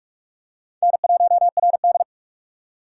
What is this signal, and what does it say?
Morse code, heard as a keyed tone, about 46 wpm, 700 Hz. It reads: N1RD
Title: USA Call Signs - 4 Characters - 2x Character Spacing - 45wpm